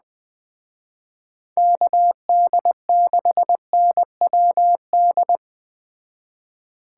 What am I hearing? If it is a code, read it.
Morse code, heard as a keyed tone, about 20 wpm, 700 Hz. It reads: KD6NWD
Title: Speed Racing - USA 2x3 Call Signs 17wpm